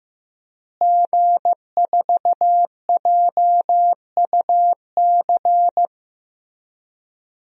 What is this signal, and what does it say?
Morse code, heard as a keyed tone, about 15 wpm, 700 Hz. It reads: G4JUC
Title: DXCC-100 Call Signs - 5 Characters - Repeated 3x - 15wpm